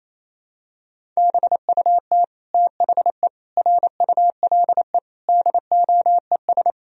Morse code, heard as a keyed tone, about 28 wpm, 700 Hz. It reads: BUT THE RULE DOES
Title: Sentences from Top 1000 Words - Review Part 2 - 28wpm